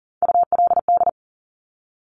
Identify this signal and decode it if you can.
Morse code, heard as a keyed tone, about 38 wpm, 690 Hz. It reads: ULD
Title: Top Three Letter Combinations in Words 40wpm